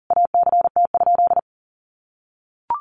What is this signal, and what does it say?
Morse code, heard as a keyed tone, about 37 wpm, 700 Hz. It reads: ACT?
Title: Sentences from Top 1400 Words 40wpm